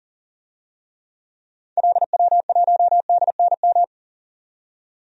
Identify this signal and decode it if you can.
Morse code, heard as a keyed tone, about 40 wpm, 700 Hz. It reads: LW1DNM
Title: DXCC-100 Call Signs - 6 Characters - 40wpm